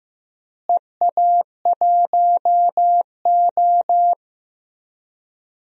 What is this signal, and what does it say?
Morse code, heard as a keyed tone, about 15 wpm, 700 Hz. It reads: EA1O
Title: DXCC-100 Call Signs - 4 Characters - Repeated 2x - 15wpm